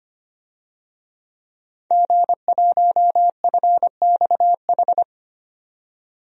Morse code, heard as a keyed tone, about 25 wpm, 700 Hz. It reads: G1FXH
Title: DXCC-100 Call Signs - 5 Characters - 25wpm